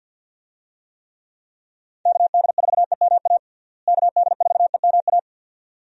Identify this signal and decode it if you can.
Morse code, heard as a keyed tone, about 47 wpm, 700 Hz. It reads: KD4EMA KD4EMA
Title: USA Call Signs - 6 Characters - Repeated 2x - 50wpm